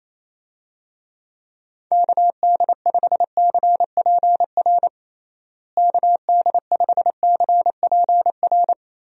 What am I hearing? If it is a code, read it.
Morse code, heard as a keyed tone, about 28 wpm, 700 Hz. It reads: KD5CPR KD5CPR
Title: USA Call Signs - 6 Characters - Repeated 2x - 28wpm